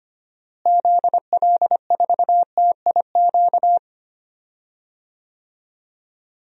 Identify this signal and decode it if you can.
Morse code, heard as a keyed tone, about 25 wpm, 690 Hz. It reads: ZL4TIQ
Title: DXCC-100 Call Signs - 6 Characters - Repeated 3x - 25wpm